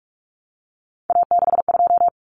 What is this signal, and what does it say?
Morse code, heard as a keyed tone, about 44 wpm, 700 Hz. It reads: A62
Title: Common Cut Numbers - 3-Digit 45wpm